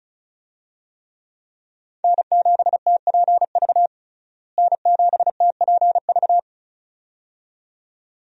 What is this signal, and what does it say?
Morse code, heard as a keyed tone, about 36 wpm, 700 Hz. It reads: N7TPV N7TPV
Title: USA Call Signs - 5 Characters - Repeated 2x - 35wpm